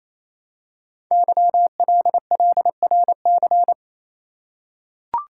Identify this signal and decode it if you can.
Morse code, heard as a keyed tone, about 29 wpm, 700 Hz. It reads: YLLRC
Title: Code Groups - 5 Letters - 28wpm